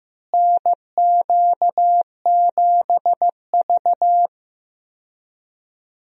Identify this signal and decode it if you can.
Morse code, heard as a keyed tone, about 15 wpm, 700 Hz. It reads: NQ7V
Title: USA Call Signs - 4 Characters - Repeated 3x - 15wpm